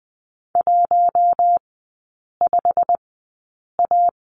Morse code, heard as a keyed tone, about 20 wpm, spaced as for 8 wpm, 700 Hz. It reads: 15A
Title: 093 Lesson 16e - Letter c - Three Characters 20wpm